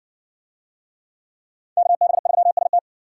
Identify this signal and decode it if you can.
Morse code, heard as a keyed tone, about 54 wpm, 700 Hz. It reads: KB3ST